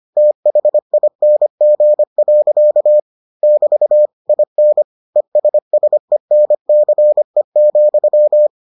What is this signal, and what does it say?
Morse code, heard as a keyed tone, about 25 wpm, 600 Hz. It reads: THING. <BT> IN ESSENCE,